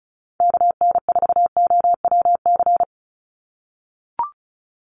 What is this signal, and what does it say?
Morse code, heard as a keyed tone, about 36 wpm, 700 Hz. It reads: KN4OWC
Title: USA Call Signs 35wpm